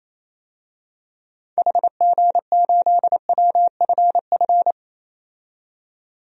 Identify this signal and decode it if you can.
Morse code, heard as a keyed tone, about 28 wpm, 700 Hz. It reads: HG8WFF
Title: DXCC-100 Call Signs - 6 Characters - 28wpm